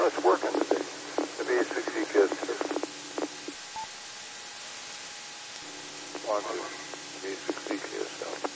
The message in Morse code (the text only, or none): none